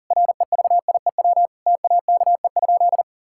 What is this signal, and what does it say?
Morse code, heard as a keyed tone, about 37 wpm, 700 Hz. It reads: REVIEW TAKE?
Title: Sentences from Top 1000 Words - Review Part 6 - 40wpm